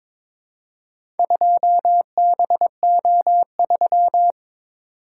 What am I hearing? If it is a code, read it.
Morse code, heard as a keyed tone, about 22 wpm, 700 Hz. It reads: 2BO3
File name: practice / 22wpm / Four Letter-Number Combinations - 22wpm